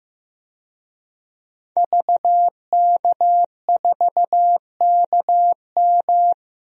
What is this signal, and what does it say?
Morse code, heard as a keyed tone, about 15 wpm, 700 Hz. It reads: VK4KM
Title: DXCC-100 Call Signs - 5 Characters - 15wpm